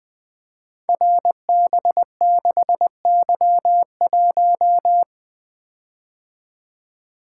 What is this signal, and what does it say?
Morse code, heard as a keyed tone, about 20 wpm, 700 Hz. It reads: RB6Y1
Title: Code Groups - 5 Letter:Numbers - 20wpm